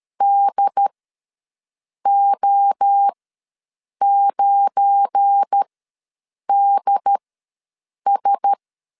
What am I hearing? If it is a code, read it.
Morse code, heard as a keyed tone, about 13 wpm, spaced as for 7 wpm, 800 Hz. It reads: DO9DS
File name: LEZIONI / 7 GF9Z / 7 GF9 ws 12wpm 3344